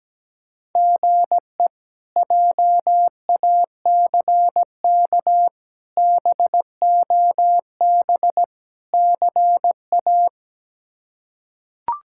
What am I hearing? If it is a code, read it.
Morse code, heard as a keyed tone, about 17 wpm, 700 Hz. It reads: GE JACK BOB CA
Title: K1USN SST QSO Practice - 17wpm